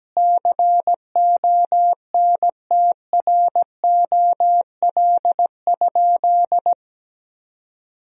Morse code, heard as a keyed tone, about 17 wpm, 700 Hz. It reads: CONTROL?
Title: Sentences from Top 1000 Words - Review Part 3 - 17wpm